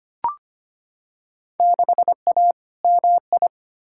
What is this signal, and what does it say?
Morse code, heard as a keyed tone, about 25 wpm, 700 Hz. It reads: 6A MI
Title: Field Day Exchanges - 25wpm